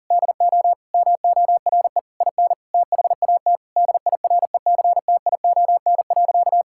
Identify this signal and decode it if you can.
Morse code, heard as a keyed tone, about 40 wpm, 700 Hz. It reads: DO MORE IN THAT DIRECTION.
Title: Sentences from Top 1000 Words - Review Part 5 - 40wpm